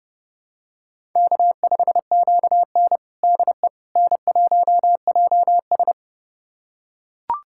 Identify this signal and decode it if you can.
Morse code, heard as a keyed tone, about 30 wpm, 700 Hz. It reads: K5QN DE N1JS
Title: USA Callsign DE Callsign - 4 Characters - 30wpm